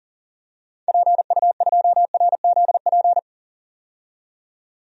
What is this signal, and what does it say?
Morse code, heard as a keyed tone, about 37 wpm, 700 Hz. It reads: PU2RZP
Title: DXCC-100 Call Signs - 6 Characters - 40wpm